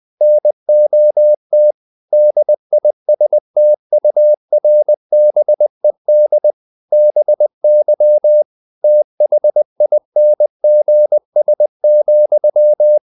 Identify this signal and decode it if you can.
Morse code, heard as a keyed tone, about 20 wpm, 600 Hz. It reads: NOT DISTURBED BY THINGS,
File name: CW / Practice / EnglishQuotations0320